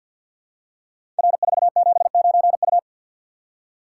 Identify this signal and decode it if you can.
Morse code, heard as a keyed tone, about 48 wpm, 700 Hz. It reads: R479U
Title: Code Groups - 5 Letter:Numbers - 50wpm